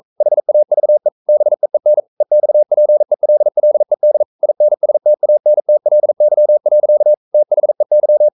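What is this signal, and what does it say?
Morse code, heard as a keyed tone, about 42 wpm, 600 Hz. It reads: HAVE BEEN EXPELLED INSTANTLY. THEY
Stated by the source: Morse code sending of an English novel at 42 wpm.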